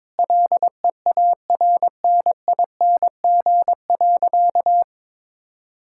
Morse code, heard as a keyed tone, about 22 wpm, 700 Hz. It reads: LEARNING.
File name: Sentences with 7-Word Phrases - 22wpm